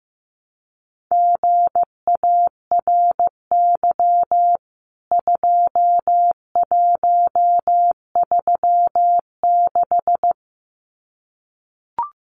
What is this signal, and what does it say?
Morse code, heard as a keyed tone, about 15 wpm, 700 Hz. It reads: GARY 2136
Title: CWT Exchange 15wpm